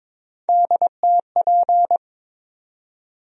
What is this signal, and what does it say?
Morse code, heard as a keyed tone, about 22 wpm, 700 Hz. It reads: DTP
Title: USA Call Sign Postfixes - 3 Letters 22wpm